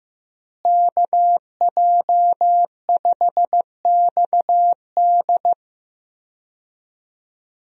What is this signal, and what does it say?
Morse code, heard as a keyed tone, about 15 wpm, 700 Hz. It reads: KJ5XD
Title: USA Call Signs - 5 Characters - Repeated 2x - 15wpm